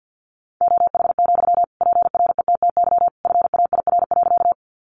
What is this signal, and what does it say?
Morse code, heard as a keyed tone, about 50 wpm, 700 Hz. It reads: OH, PRETTY FAIR.
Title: Sentences from Top 1300 Words 50wpm